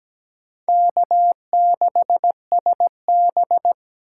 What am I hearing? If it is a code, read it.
Morse code, heard as a keyed tone, about 17 wpm, 700 Hz. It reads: K6SB
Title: USA Call Signs - 6x Character Spacing - 17wpm